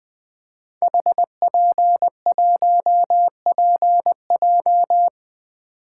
Morse code, heard as a keyed tone, about 20 wpm, 700 Hz. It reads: HP1PJ
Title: Code Groups - 5 Letter:Numbers - 20wpm